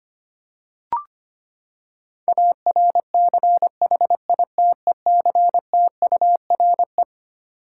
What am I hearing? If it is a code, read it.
Morse code, heard as a keyed tone, about 25 wpm, 700 Hz. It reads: ARCHITECTURE
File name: Twelve Letter Words - 25wpm